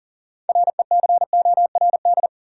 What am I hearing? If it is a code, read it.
Morse code, heard as a keyed tone, about 40 wpm, 700 Hz. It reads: RECORD
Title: Six Letter Words - 40wpm